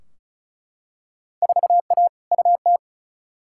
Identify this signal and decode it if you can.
Morse code, heard as a keyed tone, about 34 wpm, 700 Hz. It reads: 4A UT
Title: Field Day Exchanges - 35wpm